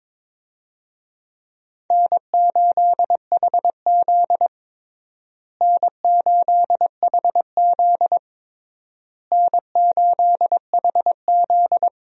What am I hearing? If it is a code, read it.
Morse code, heard as a keyed tone, about 22 wpm, 700 Hz. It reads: N8HZ N8HZ N8HZ
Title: USA Call Signs - 4 Characters - Repeated 3x - 22wpm